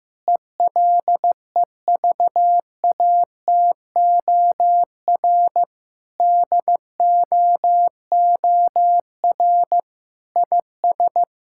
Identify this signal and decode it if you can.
Morse code, heard as a keyed tone, about 16 wpm, 700 Hz. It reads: ELEVATOR DOOR IS
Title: Sentences from Top 1600 Words 15wpm